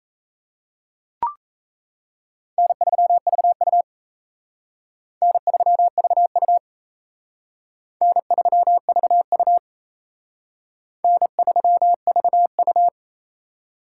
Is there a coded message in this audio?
Yes: Morse code from 2.6 to 12.9 s, heard as a keyed tone, about 35 wpm, 700 Hz. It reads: N3VU N3VU N3VU N3VU